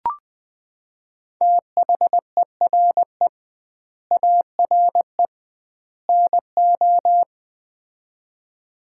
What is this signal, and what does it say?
Morse code, heard as a keyed tone, about 20 wpm, 700 Hz. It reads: THERE ARE NO
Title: 2x Word-Spacing - Sets of 3 Words - 20wpm